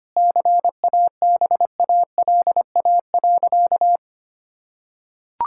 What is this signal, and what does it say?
Morse code, heard as a keyed tone, about 25 wpm, 700 Hz. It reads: CABALA.
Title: Pangram Challenge v2 - 25wpm